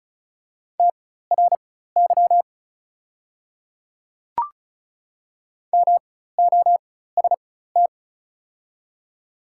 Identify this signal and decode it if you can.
Morse code, heard as a keyed tone, about 36 wpm, spaced as for 16 wpm, 700 Hz. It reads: TRY MOST
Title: 4x Character Spacing - Top 200 Words - 35wpm